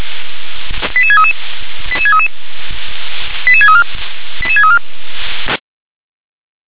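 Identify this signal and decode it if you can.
Five-tone selective call (ZVEI), heard as quick runs of five tones, tones of about 70 ms. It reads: 88633 88633 88634 88634
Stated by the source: ZVEI five-tone selective call received on SDR.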